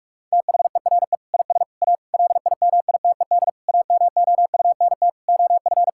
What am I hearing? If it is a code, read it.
Morse code, heard as a keyed tone, about 45 wpm, 700 Hz. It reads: THERE IS A LIMITED AMOUNT OF
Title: Sentences from Top 1000 Words - Review Part 6 - 45wpm